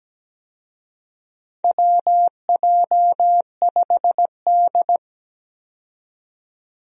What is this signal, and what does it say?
Morse code, heard as a keyed tone, about 17 wpm, 700 Hz. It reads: WJ5D
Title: USA Call Signs - 4 Characters - 17wpm